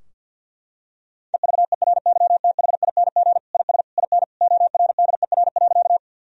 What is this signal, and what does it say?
Morse code, heard as a keyed tone, about 51 wpm, 700 Hz. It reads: EVERYTHING IS IN ORDER.
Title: Sentences from Top 1000 Words - Review Part 2 - 50wpm